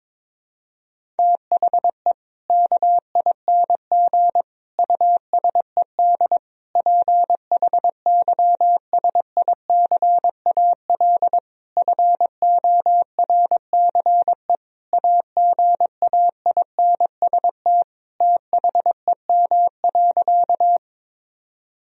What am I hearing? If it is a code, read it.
Morse code, heard as a keyed tone, about 22 wpm, 700 Hz. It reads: THE KING USED PHYSICAL FORCE AGAINST THEM.
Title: Sentences from Top 1000 Words - Review Part 4 - 22wpm